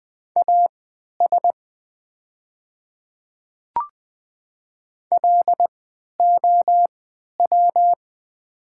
Morse code, heard as a keyed tone, about 20 wpm, spaced as for 11 wpm, 690 Hz. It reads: AS LOW
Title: 3x Character Spacing - Top 200 Words - 20wpm